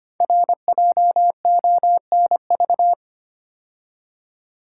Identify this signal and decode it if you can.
Morse code, heard as a keyed tone, about 25 wpm, 700 Hz. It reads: RJONV
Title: Code Groups - 5 Letters - 25wpm